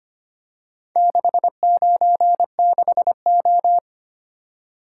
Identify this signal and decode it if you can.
Morse code, heard as a keyed tone, about 25 wpm, 700 Hz. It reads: 696O
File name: practice / 25wpm / Four Letter-Number Combinations - 25wpm